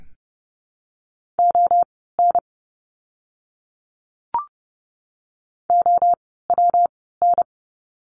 Morse code, heard as a keyed tone, about 30 wpm, spaced as for 17 wpm, 700 Hz. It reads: ON OWN
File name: course / 30:10wpm / 230 Lesson 37f - Letter x - Words 30wpm